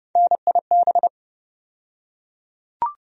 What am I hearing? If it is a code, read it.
Morse code, heard as a keyed tone, about 30 wpm, 700 Hz. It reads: NIB
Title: USA Call Sign Postfixes - 3 Letters 30wpm